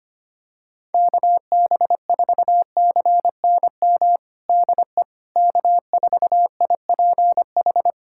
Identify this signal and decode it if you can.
Morse code, heard as a keyed tone, about 25 wpm, 700 Hz. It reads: KB4CNM DE K4IPH
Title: USA Callsign DE Callsign 25wpm